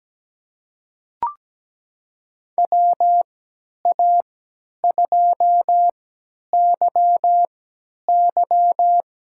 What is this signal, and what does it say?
Morse code, heard as a keyed tone, about 17 wpm, spaced as for 10 wpm, 700 Hz. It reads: WA2YY
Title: USA Call Signs - 3x Character Spacing - 17wpm